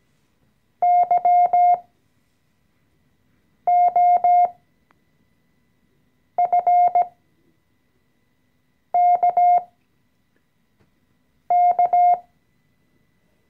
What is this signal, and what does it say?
Morse code, heard as a keyed tone, about 17 wpm, spaced as for 4 wpm, 710 Hz. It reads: YOFKK
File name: CW-Online-Kurs / Anfänger / Lektion 014